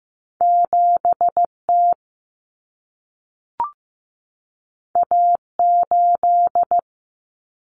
Morse code, heard as a keyed tone, about 15 wpm, 700 Hz. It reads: 7T A8
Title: Common Cut Numbers - 2-Digit 15wpm